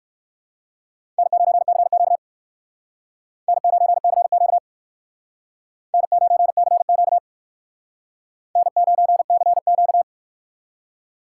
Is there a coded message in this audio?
Yes: Morse code from 1.2 to 10.0 s, heard as a keyed tone, about 62 wpm, 700 Hz. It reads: N9CQ N9CQ N9CQ N9CQ